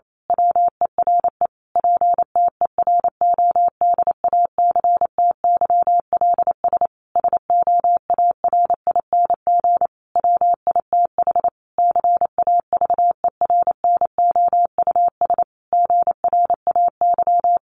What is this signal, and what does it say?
Morse code, heard as a keyed tone, about 28 wpm, 700 Hz. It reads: WERE PTERODACTYLS SOARING WITH CAVERNOUS GRAY
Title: A Sound Of Thunder 28wpm